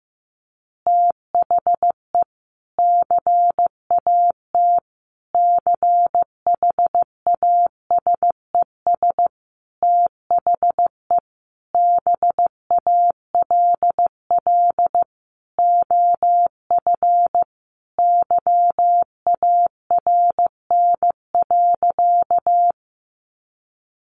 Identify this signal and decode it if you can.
Morse code, heard as a keyed tone, about 15 wpm, 700 Hz. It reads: THE CAT CHASES THE BALL OF YARN.
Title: Additional Easy Phrases 15wpm